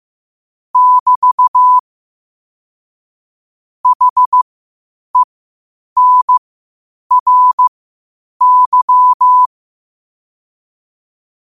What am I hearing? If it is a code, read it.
Morse code, heard as a keyed tone, about 15 wpm, spaced as for 8 wpm, 1000 Hz. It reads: <BT> HENRY
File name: qotdfiles / 8wpm